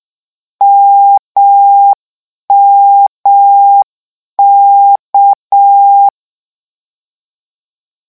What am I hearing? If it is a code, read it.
Morse code, heard as a keyed tone, about 6 wpm, 800 Hz. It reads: MMK
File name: lecon 1-vitesse 6